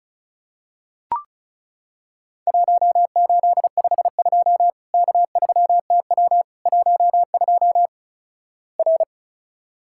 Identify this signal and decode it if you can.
Morse code, heard as a keyed tone, about 36 wpm, 700 Hz. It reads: 1852 K3TW 12 R
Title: Worked All Europe DX Contest - QTC Exchange - 35wpm